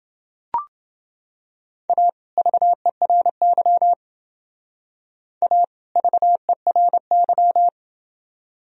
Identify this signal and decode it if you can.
Morse code, heard as a keyed tone, about 29 wpm, 700 Hz. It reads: A VERY A VERY